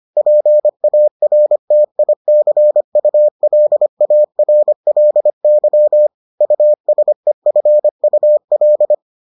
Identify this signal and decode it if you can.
Morse code, heard as a keyed tone, about 25 wpm, 600 Hz. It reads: PARTICULARLY USEFUL